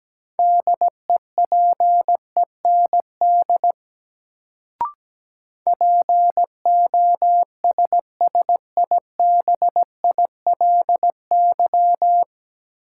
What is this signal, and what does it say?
Morse code, heard as a keyed tone, about 17 wpm, 700 Hz. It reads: DEPEND POSSIBILY
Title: Sentences from Top 1300 Words - New Word Drill Alt - 17wpm